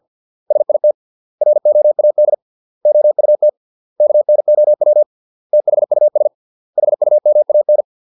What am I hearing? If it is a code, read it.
Morse code, heard as a keyed tone, about 50 wpm, 600 Hz. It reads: SIT ROAD OUT KNOW THUS HUMAN